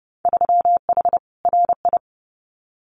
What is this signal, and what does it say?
Morse code, heard as a keyed tone, about 30 wpm, 700 Hz. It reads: 3H RI
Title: Winter Field Day Exchanges Encore 30wpm